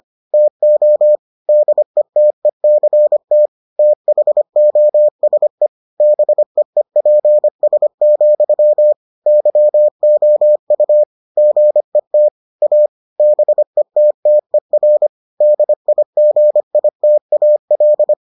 Morse code, heard as a keyed tone, about 25 wpm, 600 Hz. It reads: TO DETECT THOSE BEEPS, YOU GET A BETTER DIGITAL